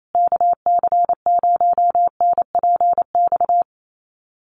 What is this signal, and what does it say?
Morse code, heard as a keyed tone, about 28 wpm, 700 Hz. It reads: KC0NPX
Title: USA Call Signs Repeated 2x 28wpm